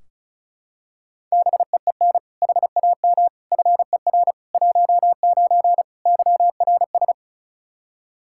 Morse code, heard as a keyed tone, about 36 wpm, 700 Hz. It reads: BEEN HAM FER 19 YRS
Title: Practice QSO - Exchange 3 - More Info - 35wpm